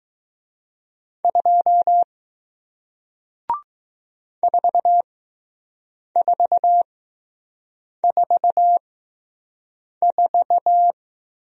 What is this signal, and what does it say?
Morse code, heard as a keyed tone, about 20 wpm, spaced as for 6 wpm, 700 Hz. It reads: 2 4444